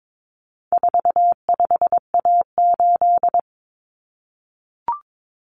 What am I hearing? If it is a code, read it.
Morse code, heard as a keyed tone, about 22 wpm, 700 Hz. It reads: 45A8
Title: Common Cut Numbers - 4-Digit 22wpm